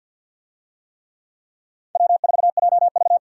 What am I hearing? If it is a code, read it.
Morse code, heard as a keyed tone, about 53 wpm, 700 Hz. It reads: W4JV
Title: USA Call Signs - 4 Characters - 50wpm